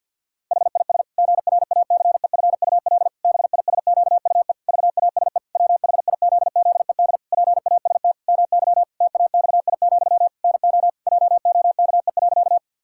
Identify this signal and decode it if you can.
Morse code, heard as a keyed tone, about 52 wpm, 700 Hz. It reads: HIS GRACEFUL BISQUE VASE WHIZZED PAST MY TAXI, NO JOKE.
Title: Pangram Challenge v2 - 50wpm